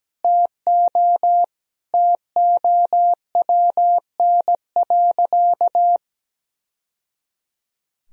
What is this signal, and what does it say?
Morse code, heard as a keyed tone, about 17 wpm, 700 Hz. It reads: TO TOWN.
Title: Sentences from Top 1000 Words - Review Part 1 - 17wpm